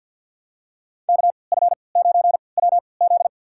Morse code, heard as a keyed tone, about 49 wpm, spaced as for 28 wpm, 700 Hz. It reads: KF9WZ
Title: USA Call Signs - 3x Character Spacing - 50wpm